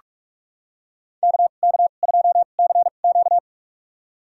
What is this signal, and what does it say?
Morse code, heard as a keyed tone, about 46 wpm, spaced as for 33 wpm, 700 Hz. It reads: KK2CQ
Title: USA Call Signs - 5 Characters - 2x Character Spacing - 45wpm